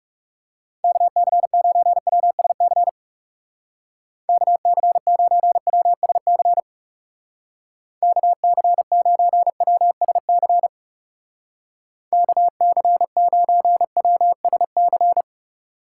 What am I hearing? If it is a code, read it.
Morse code, heard as a keyed tone, about 38 wpm, 700 Hz. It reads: KC9WSC KC9WSC KC9WSC KC9WSC